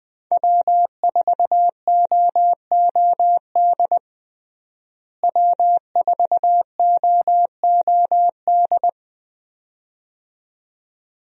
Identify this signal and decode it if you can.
Morse code, heard as a keyed tone, about 20 wpm, 700 Hz. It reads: W4OOD W4OOD
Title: USA Call Signs - 5 Characters - Repeated 2x - 20wpm